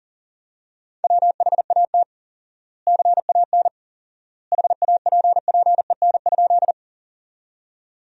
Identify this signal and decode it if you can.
Morse code, heard as a keyed tone, about 40 wpm, 700 Hz. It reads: WHAT CAN HAPPEN?
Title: 4x Word-Spacing - Sentences from Top 400 Words - 40wpm